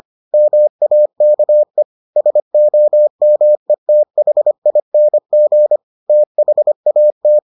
Morse code, heard as a keyed tone, about 25 wpm, 600 Hz. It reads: MAKE SOMETHING THAT